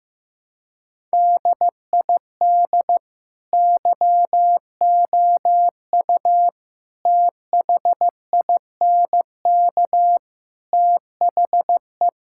Morse code, heard as a keyed tone, about 15 wpm, 700 Hz. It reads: DID YOU THINK THE
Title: Sentences from Top 1000 Words - Review Part 5 - 15wpm